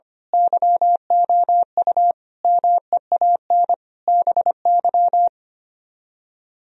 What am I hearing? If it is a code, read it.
Morse code, heard as a keyed tone, about 24 wpm, 700 Hz. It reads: YOU MEAN BY
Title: Speed-Racing - Sets of 5 Words 25wpm